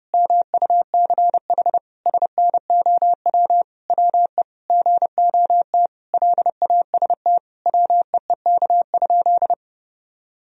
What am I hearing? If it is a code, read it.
Morse code, heard as a keyed tone, about 30 wpm, 700 Hz. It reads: MUCH SNOW WE GOT LAST WEEK?
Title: Sentences from Top 1000 Words - Review Part 3 - 30wpm